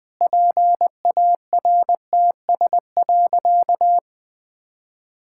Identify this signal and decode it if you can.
Morse code, heard as a keyed tone, about 20 wpm, 700 Hz. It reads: PARTS.
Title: Sentences with 6-Word Phrases - Spoken Beginning - 20wpm